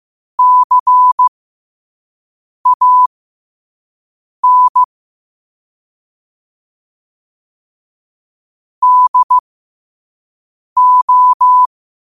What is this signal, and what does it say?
Morse code, heard as a keyed tone, about 15 wpm, spaced as for 5 wpm, 1000 Hz. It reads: CAN DO
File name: qotdfiles / 5wpm